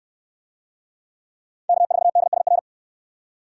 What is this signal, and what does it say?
Morse code, heard as a keyed tone, about 64 wpm, 700 Hz. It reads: K4DSR